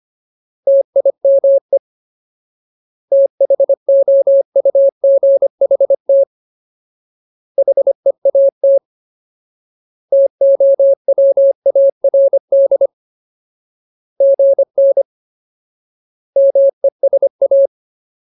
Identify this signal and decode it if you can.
Morse code, heard as a keyed tone, about 25 wpm, 550 Hz. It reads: TIME THOUGHT HEAT TOWARD GN MESA
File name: Ultimate Word List / 25WPM Ultimate